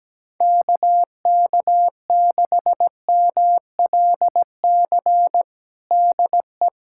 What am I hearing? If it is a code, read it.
Morse code, heard as a keyed tone, about 17 wpm, 700 Hz. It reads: KK6MLC DE
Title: USA Callsign DE Callsign 17wpm